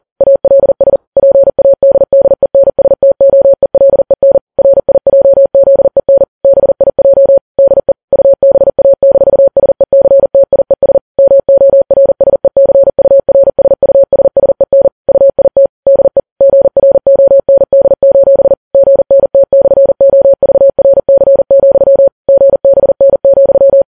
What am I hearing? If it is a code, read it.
Morse code, heard as a keyed tone, about 40 wpm, 550 Hz. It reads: ALS PADDENSTOELEN RIJZEN BIJ DE UBA-SECTIES MORSECURSUSSEN UIT DE GROND: GNT/OVRC, GBN,
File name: Files Web tls / Morse on9fv / morse40kps